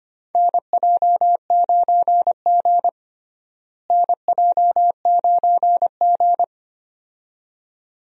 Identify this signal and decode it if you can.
Morse code, heard as a keyed tone, about 25 wpm, 700 Hz. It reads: NJ9G NJ9G
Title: USA Call Signs - 4 Characters - Repeated 2x - 25wpm